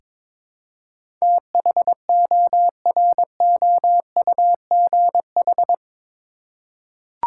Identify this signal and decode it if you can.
Morse code, heard as a keyed tone, about 21 wpm, 700 Hz. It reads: THOROUGH
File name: practice / 22wpm / Eight Letter Words - 22wpm